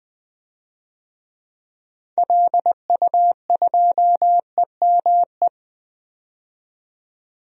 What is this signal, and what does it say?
Morse code, heard as a keyed tone, about 20 wpm, 700 Hz. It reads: LU2EME
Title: DXCC-100 Call Signs - 6 Characters - 20wpm